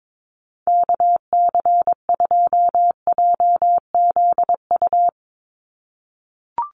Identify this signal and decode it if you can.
Morse code, heard as a keyed tone, about 22 wpm, 700 Hz. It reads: KC2JZU
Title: USA Call Signs 22wpm